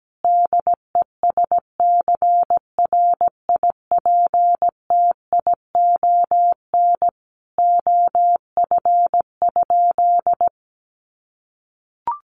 Morse code, heard as a keyed tone, about 17 wpm, 700 Hz. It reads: DESCRIPTION OF?
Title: Sentences from Top 1300 Words 17wpm